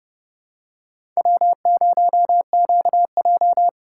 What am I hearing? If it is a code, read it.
Morse code, heard as a keyed tone, about 30 wpm, 700 Hz. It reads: W0QJ
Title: USA Call Signs - 4 Characters - 8x Character Spacing - 30wpm